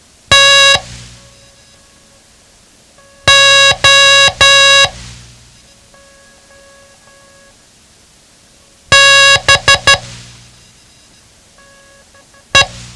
Morse code, heard as a keyed tone, about 12 wpm, spaced as for 3 wpm, 1650 Hz. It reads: TO BE
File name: ham radio / CodeQuick / Code Quick 07